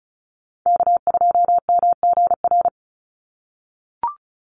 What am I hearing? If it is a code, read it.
Morse code, heard as a keyed tone, about 34 wpm, 700 Hz. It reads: K2MGR
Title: Call Signs - 35wpm